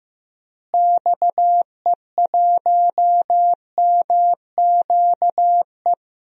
Morse code, heard as a keyed tone, about 15 wpm, 700 Hz. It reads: XE1MQE
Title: DXCC-100 Call Signs - 6 Characters - Repeated 3x - 15wpm